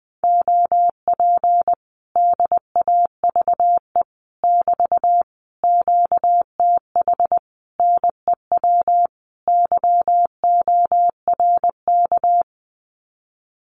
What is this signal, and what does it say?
Morse code, heard as a keyed tone, about 20 wpm, 700 Hz. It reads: OP DAVE <BT> QTH NEW YORK
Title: Common QSO Phrases Encore 20wpm